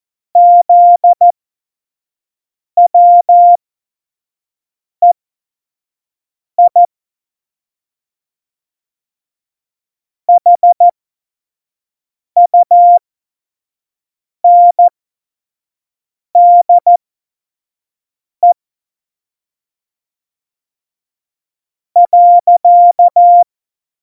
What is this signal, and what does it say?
Morse code, heard as a keyed tone, about 14 wpm, spaced as for 5 wpm, 700 Hz. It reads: ZWEI HUNDE .
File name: Morsekurs / Witze / Witz 28. August 2018, 25 F70